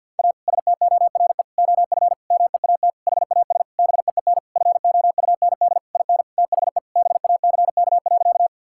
Morse code, heard as a keyed tone, about 50 wpm, 700 Hz. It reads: A STORE OF MEAT HAS BEEN FOUND IN THE BACK.
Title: Sentences from Top 1000 Words - Review Part 5 - 50wpm